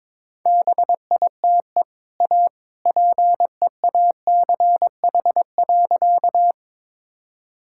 Morse code, heard as a keyed tone, about 22 wpm, 700 Hz. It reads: BITE A PEACH.
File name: practice / 22wpm / Easy Sentences with Hints - 22wpm